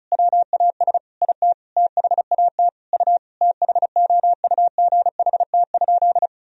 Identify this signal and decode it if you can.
Morse code, heard as a keyed tone, about 36 wpm, 700 Hz. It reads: WAS IT THAT U THOUGHT?
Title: Sentences from Top 300 Words and Abbreviations 35wpm